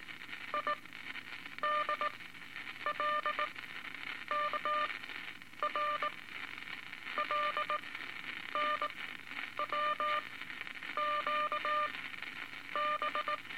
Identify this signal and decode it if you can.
Morse code, heard as a keyed tone, about 19 wpm, spaced as for 8 wpm, 1200 Hz, with noise. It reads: IDLKR LNWQB
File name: Morse Code Records / 17A LCM 162 Eight Words Per Minute